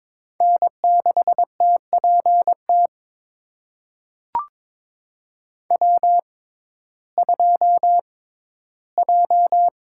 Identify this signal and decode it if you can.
Morse code, heard as a keyed tone, about 22 wpm, spaced as for 14 wpm, 700 Hz. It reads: N6TPT W2J
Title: USA Call Signs - 5 Characters - 6x Character Spacing - 22wpm